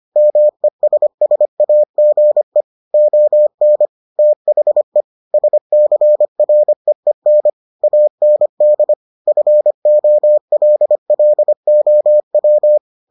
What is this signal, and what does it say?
Morse code, heard as a keyed tone, about 25 wpm, 600 Hz. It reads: MESSAGE ON THE SCREEN AND FOLLOW